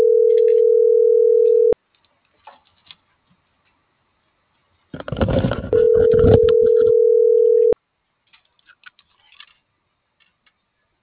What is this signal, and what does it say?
Ringback tone (North America), 440 + 480 Hz.